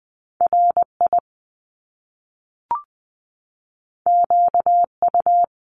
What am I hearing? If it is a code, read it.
Morse code, heard as a keyed tone, about 20 wpm, 700 Hz. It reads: RI QU
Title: Top Two Letter Combinations in Words 20wpm